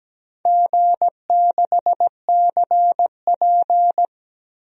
Morse code, heard as a keyed tone, about 17 wpm, 700 Hz. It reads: G6CP
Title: DXCC-100 Call Signs - 4 Characters - 17wpm